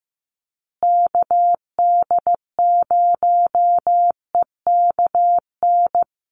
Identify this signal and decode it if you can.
Morse code, heard as a keyed tone, about 15 wpm, 700 Hz. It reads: KD0EKN
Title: USA Call Signs Repeated 3x 15wpm